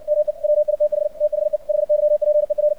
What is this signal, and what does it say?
Morse code, heard as a keyed tone, about 68 wpm, 600 Hz. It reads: ME WITH THE HYPER
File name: archived / CRSnet